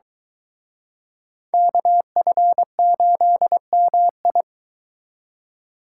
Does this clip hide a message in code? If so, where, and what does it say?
Yes: Morse code from 1.5 to 4.4 s, heard as a keyed tone, about 23 wpm, 700 Hz. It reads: KF8MI